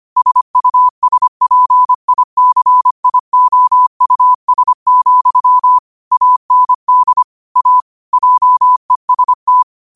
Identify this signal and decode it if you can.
Morse code, heard as a keyed tone, about 25 wpm, 1000 Hz. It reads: SUSPICIOUS, AND A JEST
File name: qotdfiles / 25wpm